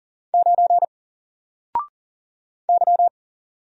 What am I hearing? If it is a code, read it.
Morse code, heard as a keyed tone, about 38 wpm, 700 Hz. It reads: 9 Y
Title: Single Character (Letter:Number:Punctuation:Prosign) 40wpm